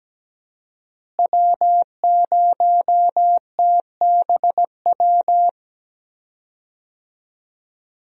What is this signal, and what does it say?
Morse code, heard as a keyed tone, about 17 wpm, 700 Hz. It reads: W0TBW
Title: USA Call Signs - 5 Characters - Repeated 3x - 17wpm